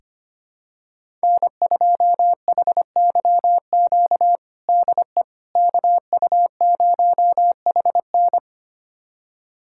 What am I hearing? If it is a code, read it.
Morse code, heard as a keyed tone, about 25 wpm, 700 Hz. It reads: N2HYQ DE KU0HN
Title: USA Callsign DE Callsign - 5 Characters - 25wpm